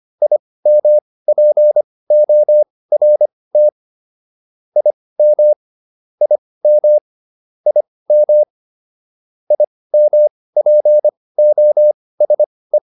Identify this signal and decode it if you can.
Morse code, heard as a keyed tone, about 25 wpm, spaced as for 18 wpm, 600 Hz. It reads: IMPORT IM IM IM IMPOSE